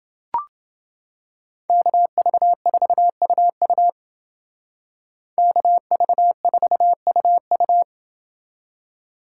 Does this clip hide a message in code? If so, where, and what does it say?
Yes: Morse code from 1.7 to 7.8 s, heard as a keyed tone, about 29 wpm, 700 Hz. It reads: KV4UU KV4UU